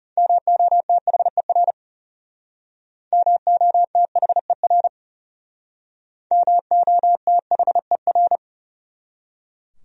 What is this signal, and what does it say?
Morse code, heard as a keyed tone, about 37 wpm, 700 Hz. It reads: MOTHER MOTHER MOTHER
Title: Speed-Racing Top 200 Words 30wpm